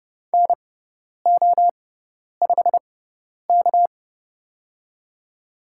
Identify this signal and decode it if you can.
Morse code, heard as a keyed tone, about 30 wpm, spaced as for 10 wpm, 700 Hz. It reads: NO5K
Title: USA Call Signs - 4 Characters - 6x Character Spacing - 30wpm